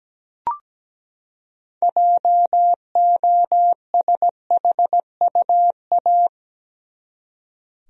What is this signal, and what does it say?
Morse code, heard as a keyed tone, about 17 wpm, 700 Hz. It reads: JOSHUA
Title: USA Common Names - 17wpm